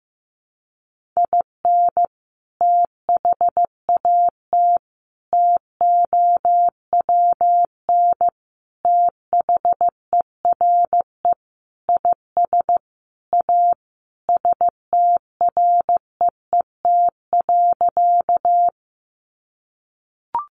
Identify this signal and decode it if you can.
Morse code, heard as a keyed tone, about 15 wpm, 700 Hz. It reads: IN THAT TOWN THERE IS A STREET.
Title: Sentences from Top 500 Words 15wpm